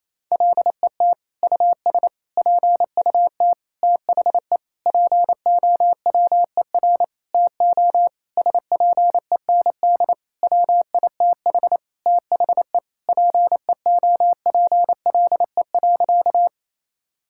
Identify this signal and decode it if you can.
Morse code, heard as a keyed tone, about 28 wpm, 700 Hz. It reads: LET US PUT THE POWER TO SPEND WITH THE PEOPLE.
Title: Sentences from Top 1000 Words - Review Part 5 - 28wpm